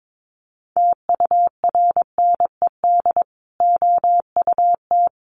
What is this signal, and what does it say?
Morse code, heard as a keyed tone, about 22 wpm, 700 Hz. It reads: TURNED OUT
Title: Sets of 5 Words Encore 22wpm